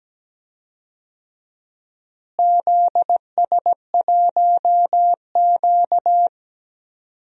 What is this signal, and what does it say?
Morse code, heard as a keyed tone, about 17 wpm, 700 Hz. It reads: ZS1Q
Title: DXCC-100 Call Signs - 4 Characters - Repeated 3x - 17wpm